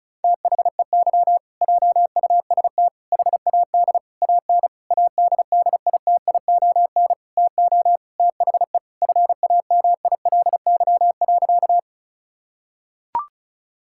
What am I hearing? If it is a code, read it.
Morse code, heard as a keyed tone, about 36 wpm, 690 Hz. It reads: THEY JUST HAD AN ADDITION TO THE FAMILY.
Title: Sentences from Top 1000 Words - Review Part 6 - 35wpm